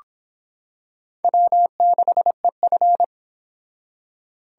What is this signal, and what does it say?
Morse code, heard as a keyed tone, about 25 wpm, 700 Hz. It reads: W6EF